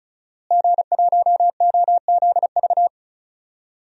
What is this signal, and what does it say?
Morse code, heard as a keyed tone, about 36 wpm, 700 Hz. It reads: G1OZV
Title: DXCC-100 Call Signs - 5 Characters - Repeated 2x - 35wpm